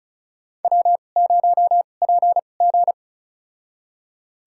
Morse code, heard as a keyed tone, about 36 wpm, spaced as for 25 wpm, 700 Hz. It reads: W0PG
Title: USA Call Signs - 4 Characters - 2x Character Spacing - 35wpm